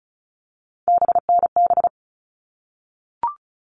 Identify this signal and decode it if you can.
Morse code, heard as a keyed tone, about 34 wpm, 700 Hz. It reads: BNB
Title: All Cut Numbers - 3-Digit 35wpm